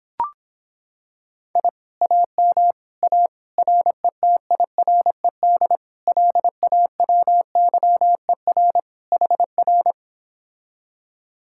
Morse code, heard as a keyed tone, about 26 wpm, 700 Hz. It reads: I AM A RETIRED LAWYER HR